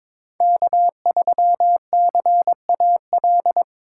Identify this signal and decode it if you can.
Morse code, heard as a keyed tone, about 22 wpm, 700 Hz. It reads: K3CAL
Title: USA Call Signs - 5 Characters - 3x Character Spacing - 22wpm